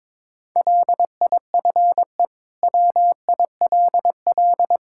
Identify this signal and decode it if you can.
Morse code, heard as a keyed tone, about 22 wpm, 700 Hz. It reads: LIFE WILL
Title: Sentences from Top 1000 Words - Review Part 5 - 22wpm